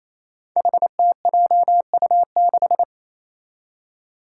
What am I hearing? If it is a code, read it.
Morse code, heard as a keyed tone, about 28 wpm, 700 Hz. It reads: HTJU6
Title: Five Challenging Letter-Number Combinations 28wpm